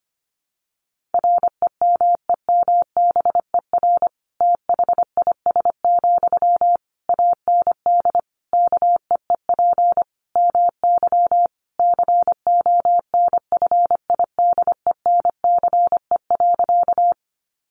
Morse code, heard as a keyed tone, about 25 wpm, 700 Hz. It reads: REMEMBER THIS, AND KEEP MY CONFIDENCE.
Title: Sentences from Top 1500 Words 25wpm